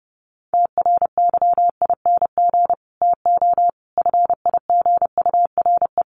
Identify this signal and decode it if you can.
Morse code, heard as a keyed tone, about 30 wpm, 700 Hz. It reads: TRYING TO FIGURE
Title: Sets of 5 Words 30wpm